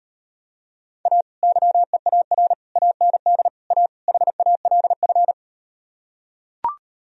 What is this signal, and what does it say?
Morse code, heard as a keyed tone, about 38 wpm, 700 Hz. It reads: A YEAR AND A HALF